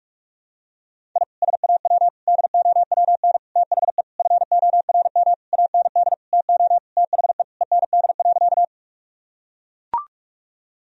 Morse code, heard as a keyed tone, about 41 wpm, 700 Hz. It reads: I SAW DOWN THE FORM AND TO THE END.
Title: Sentences from Top 1000 Words - Review Part 1 - 45wpm